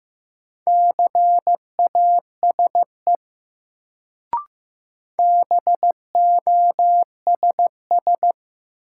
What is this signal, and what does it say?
Morse code, heard as a keyed tone, about 15 wpm, 700 Hz. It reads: CASE BOSS
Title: Sentences from Top 700 Words - New Word Drill - 15wpm